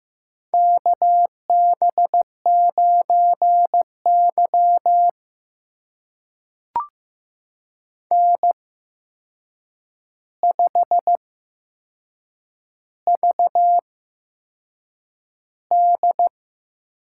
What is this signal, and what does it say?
Morse code, heard as a keyed tone, about 15 wpm, spaced as for 6 wpm, 700 Hz. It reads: KB9Y N5VD
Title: USA Call Signs - 4 Characters - 8x Character Spacing - 15wpm